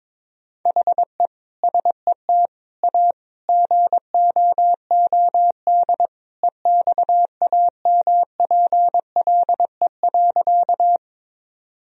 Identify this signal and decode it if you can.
Morse code, heard as a keyed tone, about 22 wpm, 700 Hz. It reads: HE SET A GOOD EXAMPLE.
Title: Sentences from Top 1000 Words - Review Part 1 - 22wpm